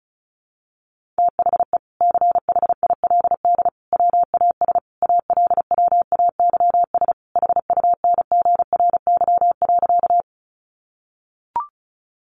Morse code, heard as a keyed tone, about 34 wpm, 700 Hz. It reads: THE CHILD WAS ALWAYS HUNGRY.
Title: Sentences from Top 1400 Words 35wpm